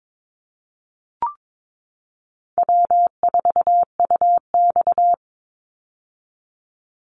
Morse code, heard as a keyed tone, about 22 wpm, 700 Hz. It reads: W4UX
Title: Call Signs - 22wpm